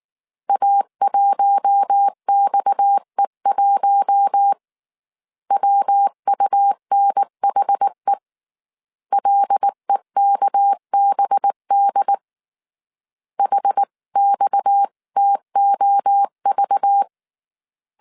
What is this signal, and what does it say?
Morse code, heard as a keyed tone, about 19 wpm, 800 Hz. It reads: A1XE1 WUNHE LEKBD HXTOV
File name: CW / DET / 1 6 1500C S C 18wpm 2356